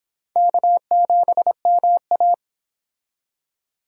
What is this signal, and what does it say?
Morse code, heard as a keyed tone, about 22 wpm, 700 Hz. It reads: K7MA